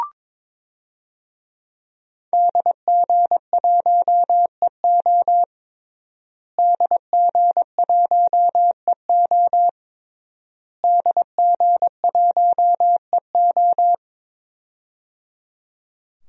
Morse code, heard as a keyed tone, about 22 wpm, 700 Hz. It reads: DG1EO DG1EO DG1EO
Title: DXCC-100 Call Signs - 5 Characters - Repeated 3x - 22wpm